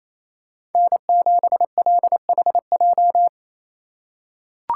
Morse code, heard as a keyed tone, about 28 wpm, 700 Hz. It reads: N7LHJ
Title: USA Call Signs - 5 Characters - 2x Character Spacing - 28wpm